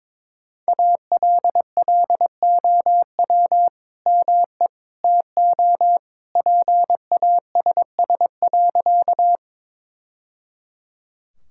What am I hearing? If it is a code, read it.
Morse code, heard as a keyed tone, about 22 wpm, 700 Hz. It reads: ALLOW ME TO PASS.
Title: Sentences from Top 1000 Words - Review Part 5 - 22wpm